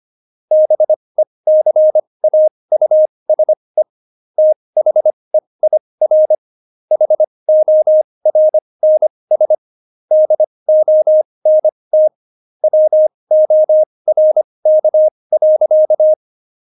Morse code, heard as a keyed tone, about 25 wpm, spaced as for 20 wpm, 620 Hz. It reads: BECAUSE THEIR HORNS DONT WORK.
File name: Jokes-25-char-20-eff